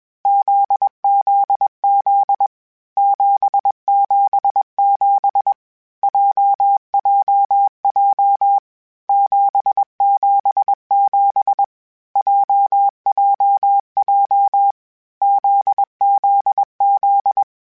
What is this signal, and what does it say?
Morse code, heard as a keyed tone, about 21 wpm, 800 Hz. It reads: ZZZ 777 JJJ 777 JJJ ZZZ
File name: Basic Sets / Basic Letter Groups Collective7 / JZ17@20